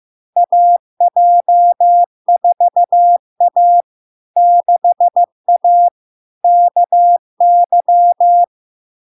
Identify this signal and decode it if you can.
Morse code, heard as a keyed tone, about 15 wpm, 700 Hz. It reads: AJ4A 6A KY
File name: LARK-FD-2022-Log-in-CW-15wpm-700-Hz